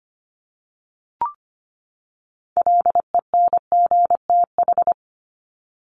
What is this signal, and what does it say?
Morse code, heard as a keyed tone, about 25 wpm, 700 Hz. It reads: LENGTH
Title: Top 751-1000 Words 25wpm